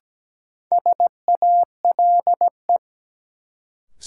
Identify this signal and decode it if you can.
Morse code, heard as a keyed tone, about 17 wpm, 700 Hz. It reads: SALE
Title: Sentences from Top 1400 Words - New Word Drill Alt - 17wpm